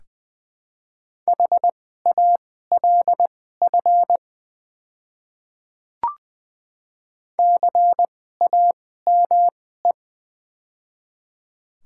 Morse code, heard as a keyed tone, about 20 wpm, spaced as for 15 wpm, 690 Hz. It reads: HALF CAME
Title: 2x Character Spacing - Top 500 Words - 20wpm